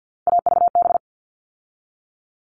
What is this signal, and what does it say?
Morse code, heard as a keyed tone, about 50 wpm, 700 Hz. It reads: AVB
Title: All Cut Numbers - 3-Digit 50wpm